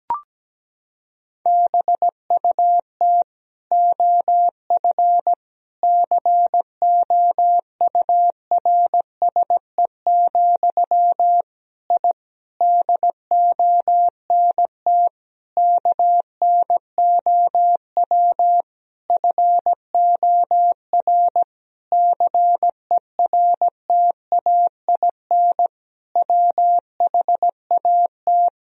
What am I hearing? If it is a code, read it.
Morse code, heard as a keyed tone, about 17 wpm, 700 Hz. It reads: BUT OF COURSE, I DONT KNOW FOR CERTAIN WHAT
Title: Sentences from Top 1000 Words - Review Part 2 - 17wpm